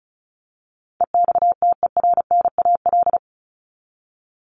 Morse code, heard as a keyed tone, about 34 wpm, 700 Hz. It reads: EXTERNAL
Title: Top 1751-2000 Words 35wpm